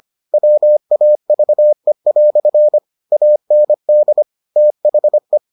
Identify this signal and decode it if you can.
Morse code, heard as a keyed tone, about 25 wpm, 600 Hz. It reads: WAVE" AND THE